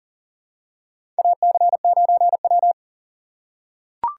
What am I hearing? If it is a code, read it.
Morse code, heard as a keyed tone, about 40 wpm, 690 Hz. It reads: AC9W
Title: USA Call Signs - 4 Characters - 2x Character Spacing - 40wpm